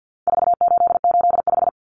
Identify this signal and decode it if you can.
Morse code, heard as a keyed tone, about 50 wpm, 700 Hz. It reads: 4885
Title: Common Cut Numbers - 4-Digit 50wpm